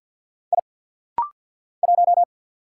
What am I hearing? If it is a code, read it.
Morse code, heard as a keyed tone, about 49 wpm, 700 Hz. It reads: I 1
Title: Single Letter-Number - Rapid-Fire 50wpm